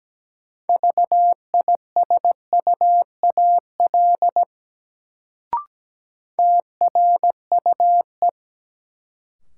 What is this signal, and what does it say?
Morse code, heard as a keyed tone, about 17 wpm, 700 Hz. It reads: VISUAL TRUE
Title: Sets of 5 Words Encore - New Word Drill - 17wpm